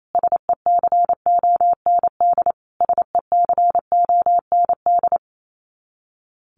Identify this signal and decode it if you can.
Morse code, heard as a keyed tone, about 28 wpm, 700 Hz. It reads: SECOND SECOND
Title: Top 500 Words Repeated 28wpm